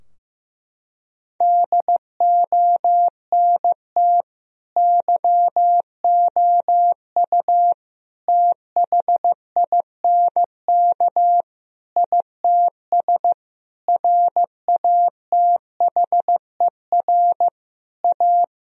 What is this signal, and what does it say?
Morse code, heard as a keyed tone, about 15 wpm, 700 Hz. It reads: DONT YOU THINK ITS RATHER A
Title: Sentences from Top 1000 Words - Review Part 4 - 15wpm